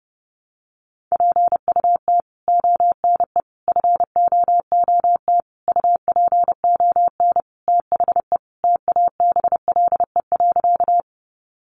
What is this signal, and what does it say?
Morse code, heard as a keyed tone, about 30 wpm, 700 Hz. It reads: PUT ONE FOOT UPON THE TABLE.
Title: Sentences from Top 1100 Words 30wpm